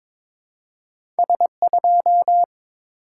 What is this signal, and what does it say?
Morse code, heard as a keyed tone, about 22 wpm, 700 Hz. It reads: S2
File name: practice / 22wpm / Two Letter-Number Combinations - 22wpm